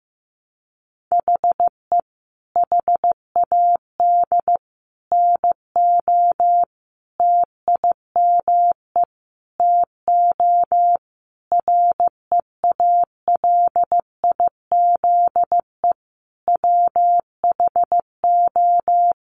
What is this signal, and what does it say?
Morse code, heard as a keyed tone, about 15 wpm, 700 Hz. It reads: HE HAD NO TIME TO REALIZE WHO
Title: Sentences from Top 1000 Words 15wpm